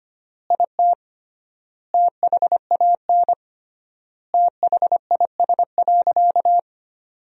Morse code, heard as a keyed tone, about 25 wpm, 700 Hz. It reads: IT THAN THIS.
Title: 3x Word-Spacing - Sentences from Top 100 Words - 25wpm